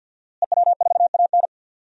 Morse code, heard as a keyed tone, about 51 wpm, 700 Hz. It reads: EW4AN
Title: DXCC-100 Call Signs - 5 Characters - Repeated 3x - 50wpm